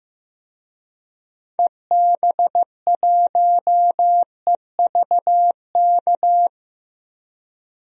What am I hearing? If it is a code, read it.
Morse code, heard as a keyed tone, about 15 wpm, 700 Hz. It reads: EB1EVK
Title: DXCC-100 Call Signs - 6 Characters - Repeated 2x - 15wpm